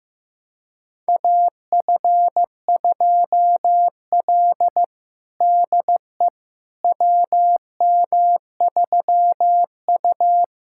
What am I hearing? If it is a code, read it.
Morse code, heard as a keyed tone, about 15 wpm, 700 Hz. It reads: AF2L DE WM3U
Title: USA Callsign DE Callsign - 4 Characters - 15wpm